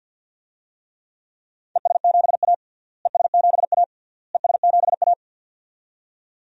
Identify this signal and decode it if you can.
Morse code, heard as a keyed tone, about 49 wpm, 700 Hz. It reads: ES7A ES7A ES7A
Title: DXCC-100 Call Signs - 4 Characters - Repeated 3x - 50wpm